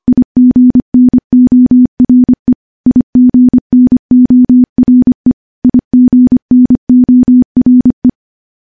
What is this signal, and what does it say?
Morse code, heard as a keyed tone, about 25 wpm, 250 Hz. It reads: IGNORE IGNORE IGNORE
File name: morse / morse-training-25WPM